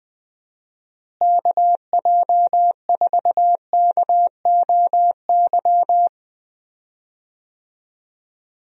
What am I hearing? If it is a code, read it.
Morse code, heard as a keyed tone, about 20 wpm, 700 Hz. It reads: KJ4KOY
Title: USA Call Signs - 6 Characters - Repeated 3x - 20wpm